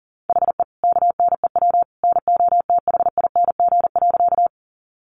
Morse code, heard as a keyed tone, about 40 wpm, 700 Hz. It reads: HE KNEW NOTHING.
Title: Sentences from Top 500 Words 40wpm